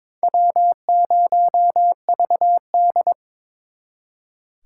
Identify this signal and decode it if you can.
Morse code, heard as a keyed tone, about 26 wpm, 700 Hz. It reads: W0VD
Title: Speed Racing - USA 1x2 Call Signs 22wpm